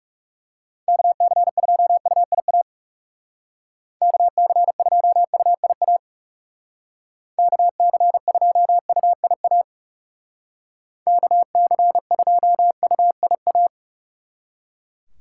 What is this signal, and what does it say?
Morse code, heard as a keyed tone, about 37 wpm, 700 Hz. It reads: KC2UIA KC2UIA KC2UIA KC2UIA
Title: Speed Racing - USA 2x3 Call Signs 30wpm